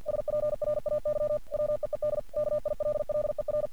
Morse code, heard as a keyed tone, about 48 wpm, 600 Hz. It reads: SPRAY WEED KILLER
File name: archived / CRSnet